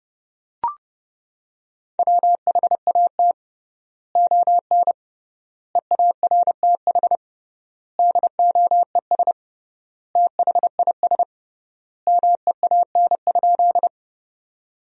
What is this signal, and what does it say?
Morse code, heard as a keyed tone, about 30 wpm, 700 Hz. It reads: WHAT ON EARTH DOES THIS MEAN?
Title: 3x Word-Spacing - Sentences from Top 200 Words - 30wpm